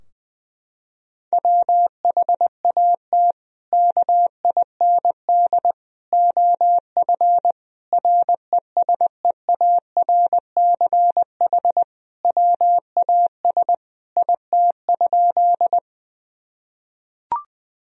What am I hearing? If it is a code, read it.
Morse code, heard as a keyed tone, about 20 wpm, 700 Hz. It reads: WHAT KIND OF RESEARCH WAS IT?
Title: Sentences from Top 1000 Words - Review Part 4 - 20wpm